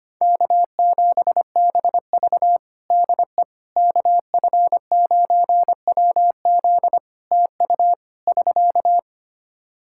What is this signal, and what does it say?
Morse code, heard as a keyed tone, about 25 wpm, 700 Hz. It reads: K7BV DE KF9WZ TU <SK>
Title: K1USN SST QSO Practice 25wpm